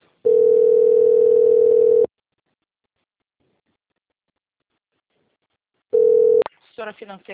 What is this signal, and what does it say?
Ringback tone (North America), 440 + 480 Hz, with speech.
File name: out- - - - - .11774